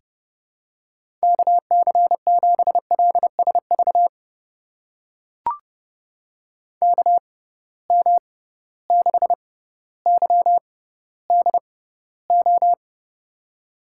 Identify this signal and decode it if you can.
Morse code, heard as a keyed tone, about 30 wpm, spaced as for 15 wpm, 700 Hz. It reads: KC7LSV KM6YDO
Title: USA Call Signs - 6 Characters - 6x Character Spacing - 30wpm